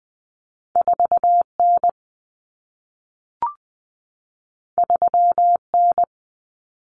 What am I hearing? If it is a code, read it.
Morse code, heard as a keyed tone, about 20 wpm, 700 Hz. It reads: 4N 3N
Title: Common Cut Numbers - 2-Digit 20wpm